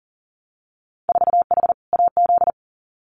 Morse code, heard as a keyed tone, about 38 wpm, 700 Hz. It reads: 4H AZ
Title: Winter Field Day Exchanges Encore 40wpm